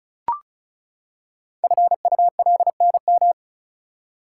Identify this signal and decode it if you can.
Morse code, heard as a keyed tone, about 36 wpm, 690 Hz. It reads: FULNM
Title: Code Groups - 5 Letters - 35wpm